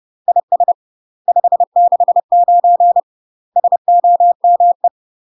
Morse code, heard as a keyed tone, about 30 wpm, 700 Hz. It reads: IS 569 SOME
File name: UK MASTER QSOs TRAINING / 30WPM / UK MASTER QSOs 0045